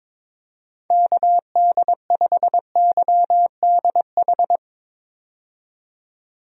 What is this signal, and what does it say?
Morse code, heard as a keyed tone, about 22 wpm, 700 Hz. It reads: KD5YDH
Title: USA Call Signs - 6 Characters - 22wpm